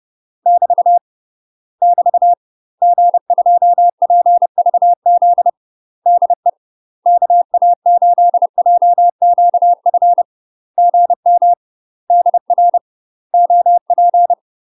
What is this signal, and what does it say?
Morse code, heard as a keyed tone, about 30 wpm, 700 Hz. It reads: <BT> <BT> G2PVZ DE KA8JQF GM DR OP
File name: UK MASTER QSOs TRAINING/30WPM